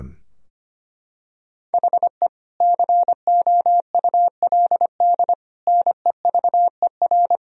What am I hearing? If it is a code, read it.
Morse code, heard as a keyed tone, about 25 wpm, 700 Hz. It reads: HE COULD NEVER
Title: Sentences from Top 1000 Words - Review Part 3 - 25wpm